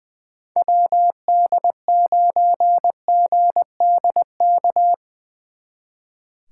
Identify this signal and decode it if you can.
Morse code, heard as a keyed tone, about 20 wpm, 700 Hz. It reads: WD9GDK
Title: Speed Racing - USA 2x3 Call Signs 20wpm